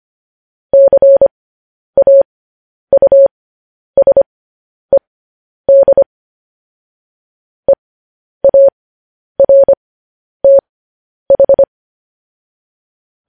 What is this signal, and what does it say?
Morse code, heard as a keyed tone, about 25 wpm, spaced as for 10 wpm, 550 Hz. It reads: CAUSED EARTH
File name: podcast / News headlines-1x-25x10